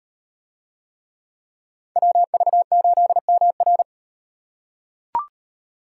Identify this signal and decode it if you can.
Morse code, heard as a keyed tone, about 38 wpm, 700 Hz. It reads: WV8MR